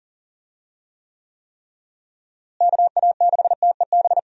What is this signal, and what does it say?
Morse code, heard as a keyed tone, about 37 wpm, 700 Hz. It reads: KA6TEB
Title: USA Call Signs - 6 Characters - Repeated 3x - 40wpm